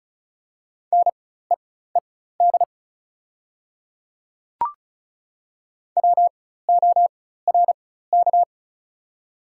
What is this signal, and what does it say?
Morse code, heard as a keyed tone, about 34 wpm, spaced as for 16 wpm, 690 Hz. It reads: NEED WORK
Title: 4x Character Spacing - Top 300 Words - 35wpm